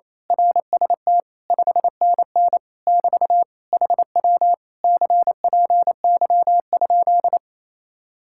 Morse code, heard as a keyed tone, about 26 wpm, 700 Hz. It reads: RST 5NN <BT> HW CPY?
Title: Speed-Racing Common QSO Phrases Encore - 28wpm